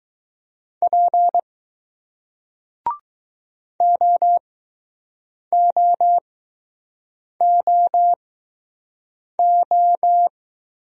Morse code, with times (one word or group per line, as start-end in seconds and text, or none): 0.8-1.4 P
3.8-10.3 OOOO